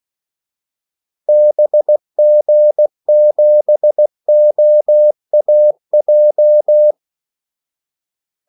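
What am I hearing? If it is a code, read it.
Morse code, heard as a keyed tone, about 16 wpm, 600 Hz. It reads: BG7OAJ
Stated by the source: Morse call sign practice at 16 wpm.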